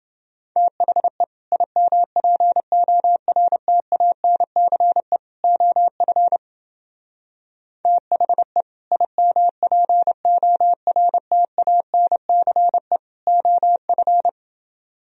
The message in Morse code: THE IMPORTANCE OF THE IMPORTANCE OF

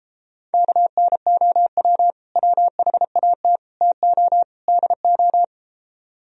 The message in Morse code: KNOW WHAT TO DO

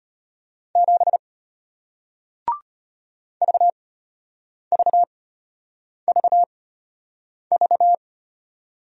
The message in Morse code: 7 VVVV

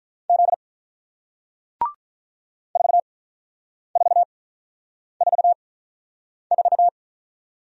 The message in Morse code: Z 4444

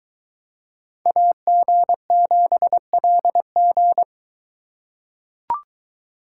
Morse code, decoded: AG7LG

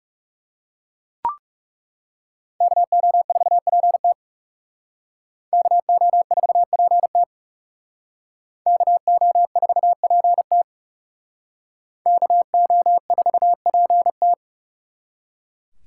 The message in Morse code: KO4PT KO4PT KO4PT KO4PT

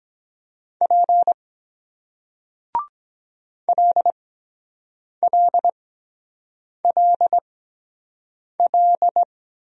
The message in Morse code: P LLLL

